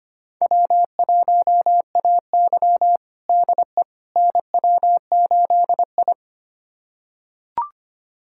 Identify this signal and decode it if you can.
Morse code, heard as a keyed tone, about 25 wpm, 700 Hz. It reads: W1AY DE NW8I
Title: USA Callsign DE Callsign - 4 Characters - 25wpm